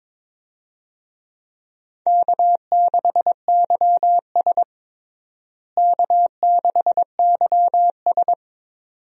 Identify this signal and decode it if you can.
Morse code, heard as a keyed tone, about 21 wpm, 700 Hz. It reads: K6YS K6YS
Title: USA Call Signs - 4 Characters - Repeated 2x - 22wpm